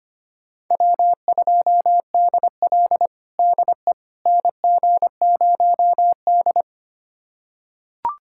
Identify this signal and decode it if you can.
Morse code, heard as a keyed tone, about 25 wpm, 700 Hz. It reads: W2DL DE NG0D
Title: USA Callsign DE Callsign - 4 Characters - 25wpm